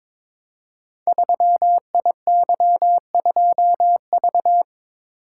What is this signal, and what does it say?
Morse code, heard as a keyed tone, about 22 wpm, 700 Hz. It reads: 3IY2V
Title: Code Groups - 5 Letter:Numbers - 22wpm